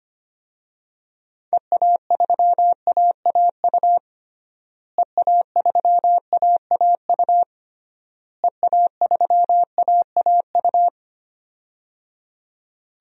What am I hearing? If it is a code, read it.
Morse code, heard as a keyed tone, about 25 wpm, 700 Hz. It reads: EA3AAU EA3AAU EA3AAU
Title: DXCC-100 Call Signs - 6 Characters - Repeated 3x - 25wpm